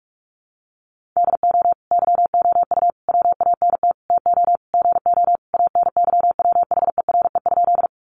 Morse code, heard as a keyed tone, about 44 wpm, 700 Hz. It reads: DO YOU WANT TO GO ANYWHERE?
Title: Sentences from Top 1400 Words 45wpm